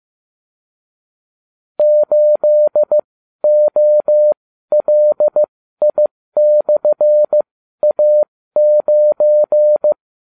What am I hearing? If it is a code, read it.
Morse code, heard as a keyed tone, about 15 wpm, spaced as for 12 wpm, 600 Hz. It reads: 8OLI/A9